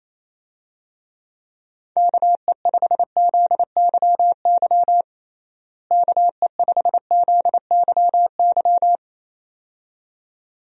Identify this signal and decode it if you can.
Morse code, heard as a keyed tone, about 28 wpm, 700 Hz. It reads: KE5ZYY KE5ZYY
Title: USA Call Signs - 6 Characters - Repeated 2x - 28wpm